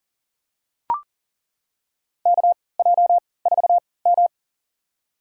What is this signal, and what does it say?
Morse code, heard as a keyed tone, about 40 wpm, spaced as for 23 wpm, 700 Hz. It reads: KJ4M
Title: USA Call Signs - 4 Characters - 3x Character Spacing - 40wpm